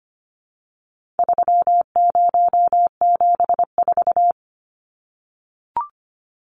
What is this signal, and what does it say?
Morse code, heard as a keyed tone, about 25 wpm, 700 Hz. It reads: 3074
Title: Numbers - 4-Digit 25wpm